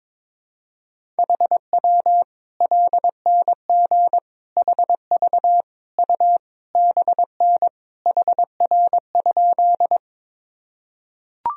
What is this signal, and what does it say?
Morse code, heard as a keyed tone, about 22 wpm, 700 Hz. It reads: HW LNG HV U BN HR?
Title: Sentences from Top 200 Words and Abbreviations 22wpm